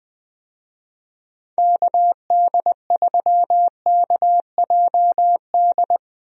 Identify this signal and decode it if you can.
Morse code, heard as a keyed tone, about 21 wpm, 700 Hz. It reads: KD3KJD
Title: USA Call Signs - 6 Characters - Repeated 2x - 20wpm